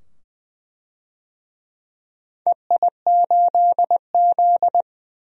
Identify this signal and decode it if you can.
Morse code, heard as a keyed tone, about 21 wpm, 700 Hz. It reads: EI8Z
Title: DXCC-100 Call Signs - 4 Characters - 20wpm